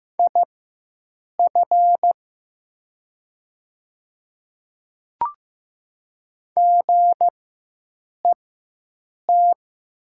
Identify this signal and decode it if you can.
Morse code, heard as a keyed tone, about 15 wpm, spaced as for 7 wpm, 700 Hz. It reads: IF GET
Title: 4x Character Spacing - Top 200 Words - 15wpm